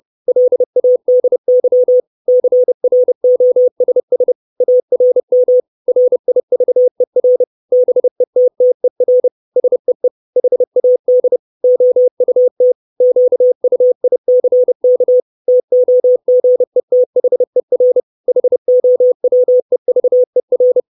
Morse code, heard as a keyed tone, about 30 wpm, 500 Hz. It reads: LADY CROSS ARM RIVER BETTER SEE HAD OUT QUICK TOGETHER HOWEVER
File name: 632 Words 30WPM